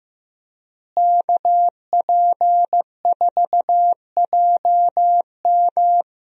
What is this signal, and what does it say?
Morse code, heard as a keyed tone, about 15 wpm, 700 Hz. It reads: KP4JM
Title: DXCC-100 Call Signs - 5 Characters - Repeated 3x - 15wpm